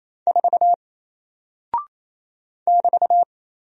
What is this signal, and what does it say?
Morse code, heard as a keyed tone, about 27 wpm, 700 Hz. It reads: 4 <BT>
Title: Single Character (Letter:Number:Punctuation:Prosign) 28wpm